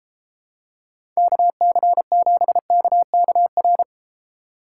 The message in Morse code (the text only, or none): KC7KKR